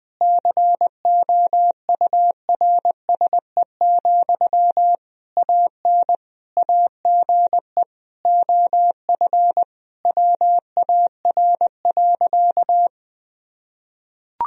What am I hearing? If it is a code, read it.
Morse code, heard as a keyed tone, about 21 wpm, 700 Hz. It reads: COURSE, AN AGE OF WAR.
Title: Sentences from Top 1000 Words - Review Part 3 - 20wpm